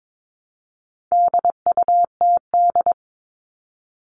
Morse code, heard as a keyed tone, about 22 wpm, 700 Hz. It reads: DUTD
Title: All Cut Numbers - 4-Digit 22wpm